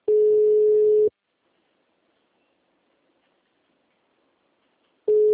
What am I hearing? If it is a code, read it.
Ringback tone (Europe, 425 Hz).
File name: out- -2001- - - .26626